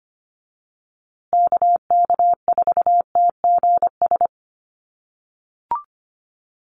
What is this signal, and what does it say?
Morse code, heard as a keyed tone, about 25 wpm, 700 Hz. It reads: KK4TGS
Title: USA Call Signs Repeated 3x 25wpm